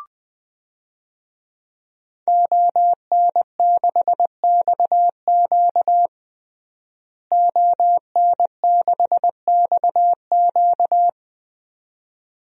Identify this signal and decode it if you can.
Morse code, heard as a keyed tone, about 20 wpm, 700 Hz. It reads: ON6XQ ON6XQ
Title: DXCC-100 Call Signs - 5 Characters - Repeated 2x - 20wpm